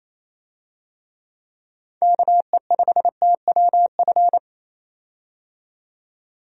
Morse code, heard as a keyed tone, about 28 wpm, 700 Hz. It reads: KE5TWF
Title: USA Call Signs - 6 Characters - 28wpm